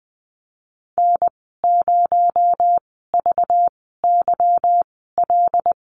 Morse code, heard as a keyed tone, about 19 wpm, spaced as for 14 wpm, 700 Hz. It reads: N0VYL
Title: 219 Lesson 35g - Number 0 - Call Signs 20wpm